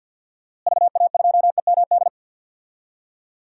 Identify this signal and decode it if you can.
Morse code, heard as a keyed tone, about 49 wpm, 700 Hz. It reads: VA2EMD
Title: DXCC-100 Call Signs - 6 Characters - Repeated 3x - 50wpm